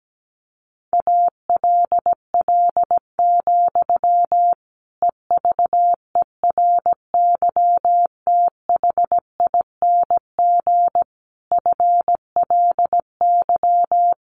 Morse code, heard as a keyed tone, about 17 wpm, 700 Hz. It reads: ALL, EVERYTHING FLY
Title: A Sound Of Thunder 17wpm